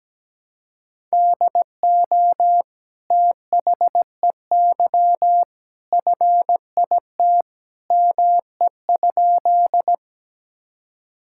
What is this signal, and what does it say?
Morse code, heard as a keyed tone, about 17 wpm, 700 Hz. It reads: DO THEY FIT ME?
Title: Sentences from Top 1000 Words - Review Part 5 - 17wpm